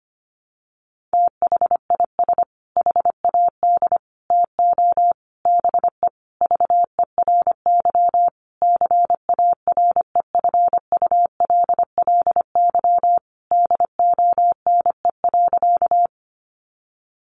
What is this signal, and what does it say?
Morse code, heard as a keyed tone, about 25 wpm, 700 Hz. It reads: THIS HAD TO BE VERY CAREFULLY DONE.
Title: Sentences from Top 1300 Words 25wpm